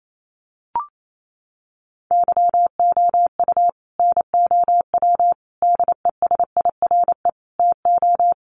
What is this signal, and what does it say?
Morse code, heard as a keyed tone, about 28 wpm, 700 Hz. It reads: YOU NOW DESIRE TO
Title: Sentences from Top 1100 Words 28wpm